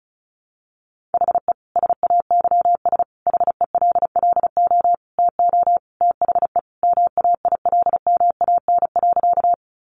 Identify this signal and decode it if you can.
Morse code, heard as a keyed tone, about 34 wpm, 700 Hz. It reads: HE SAYS HELLO TO THE MAILMAN.
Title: Easy Phrases 35wpm